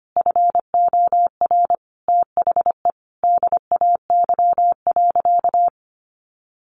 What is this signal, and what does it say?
Morse code, heard as a keyed tone, about 25 wpm, 700 Hz. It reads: FOR THE DAY.
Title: Sentences from Top 1100 Words 25wpm